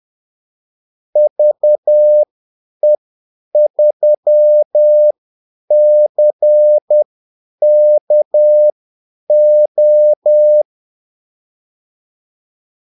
Morse code, heard as a keyed tone, about 10 wpm, spaced as for 8 wpm, 600 Hz. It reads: VE3CKO
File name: Callsigns-10-8wpm